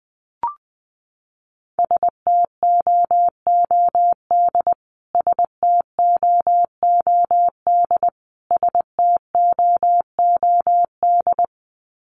Morse code, heard as a keyed tone, about 20 wpm, 700 Hz. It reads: STOOD STOOD STOOD
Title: Top 500 Words Repeated 20wpm